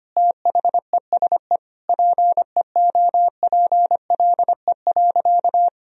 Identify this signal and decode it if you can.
Morse code, heard as a keyed tone, about 25 wpm, 700 Hz. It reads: THESE PEOPLE.
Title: Sentences from Top 1000 Words - Review Part 1 - 25wpm